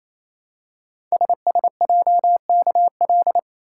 Morse code, heard as a keyed tone, about 28 wpm, 700 Hz. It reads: SSJKL
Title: Code Groups - 5 Letters - 28wpm